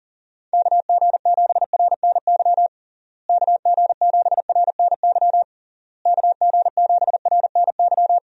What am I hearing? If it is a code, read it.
Morse code, heard as a keyed tone, about 40 wpm, 700 Hz. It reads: KG7RNY KG7RNY KG7RNY
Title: USA Call Signs - 6 Characters - Repeated 3x - 40wpm